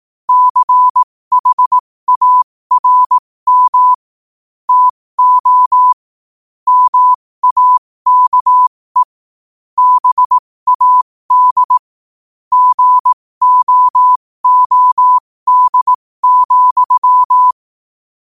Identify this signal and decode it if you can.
Morse code, heard as a keyed tone, about 18 wpm, spaced as for 15 wpm, 1000 Hz. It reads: CHARM TO MAKE BAD GOOD,
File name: qotdfiles / 15wpm